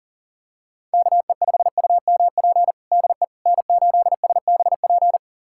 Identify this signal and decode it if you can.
Morse code, heard as a keyed tone, about 40 wpm, 700 Hz. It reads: KE5UMP DE N8SBP
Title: USA Callsign DE Callsign 40wpm